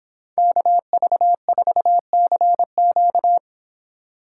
Morse code, heard as a keyed tone, about 25 wpm, 700 Hz. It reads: KV4CQ